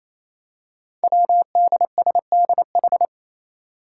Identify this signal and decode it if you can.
Morse code, heard as a keyed tone, about 27 wpm, 700 Hz. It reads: WDSDH
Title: Five Challenging Letter-Number Combinations 28wpm